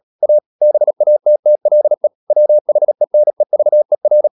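Morse code, heard as a keyed tone, about 37 wpm, 600 Hz. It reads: A BATTLE WHENEVER
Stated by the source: Morse code practice at 35–39 wpm sending English text of Ken Follett's novel Never, from CW med Gnister.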